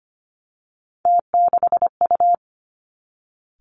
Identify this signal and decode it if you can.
Morse code, heard as a keyed tone, about 25 wpm, 700 Hz. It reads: T6U
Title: Three Challenging Letter-Number Combinations 25wpm